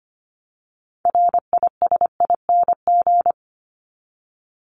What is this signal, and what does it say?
Morse code, heard as a keyed tone, about 25 wpm, 700 Hz. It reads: RISING
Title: ING Suffix - Top 100 Words with Base Words 25wpm